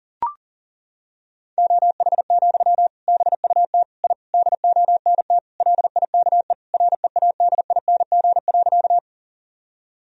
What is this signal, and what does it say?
Morse code, heard as a keyed tone, about 37 wpm, 700 Hz. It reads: OH, BUT I DONT LIKE READING.
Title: Sentences from Top 1000 Words - Review Part 3 - 40wpm